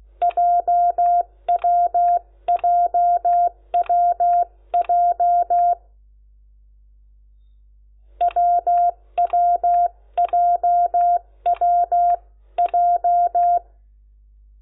Morse code, heard as a keyed tone, about 16 wpm, 700 Hz. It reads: JWJWJ WWJWJ
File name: Lecciones / CLASE Nº10